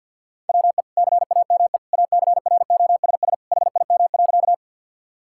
Morse code, heard as a keyed tone, about 50 wpm, 700 Hz. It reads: WE CAME ACROSS HIM.
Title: Sentences from Top 1000 Words - Review Part 5 - 50wpm